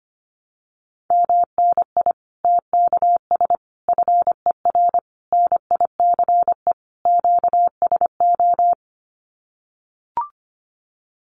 Morse code, heard as a keyed tone, about 25 wpm, 700 Hz. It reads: MNI TKS FER NICE QSO
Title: Common QSO Phrases 25wpm